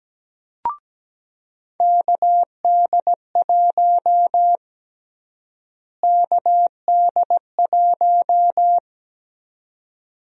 Morse code, heard as a keyed tone, about 17 wpm, 700 Hz. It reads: KD1 KD1
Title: USA Call Sign Prefixes - Repeated 2x - 17wpm